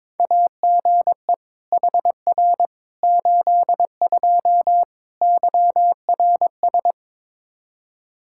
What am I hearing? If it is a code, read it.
Morse code, heard as a keyed tone, about 22 wpm, 700 Hz. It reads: AGE HR 82 YRS
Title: K1USN SST QSO Practice 22wpm